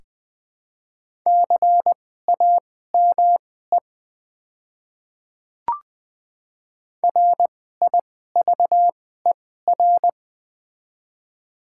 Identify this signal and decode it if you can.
Morse code, heard as a keyed tone, about 20 wpm, spaced as for 14 wpm, 700 Hz. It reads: CAME RIVER
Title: 2x Character Spacing - Top 300 Words - 20wpm